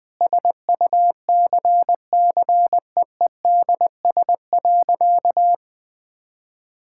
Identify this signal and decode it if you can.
Morse code, heard as a keyed tone, about 19 wpm, 700 Hz. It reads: SUCCEEDS.
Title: Sentences with 8-Word Phrases - Spoken Beginning - 20wpm